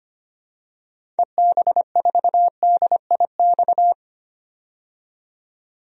Morse code, heard as a keyed tone, about 25 wpm, 700 Hz. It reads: EB4DIX
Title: DXCC-100 Call Signs - 6 Characters - 25wpm